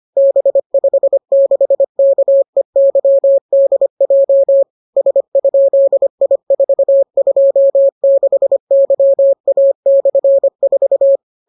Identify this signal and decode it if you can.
Morse code, heard as a keyed tone, about 25 wpm, 550 Hz. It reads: B56KEYDJ S?I426YA/4